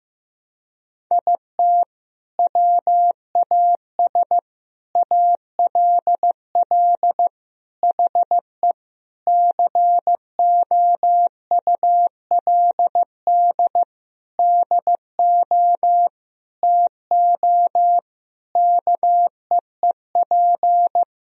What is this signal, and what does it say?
Morse code, heard as a keyed tone, about 15 wpm, 700 Hz. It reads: IT WAS ALL HE COULD DO TO KEEP
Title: Sentences from Top 1000 Words - Review Part 1 - 15wpm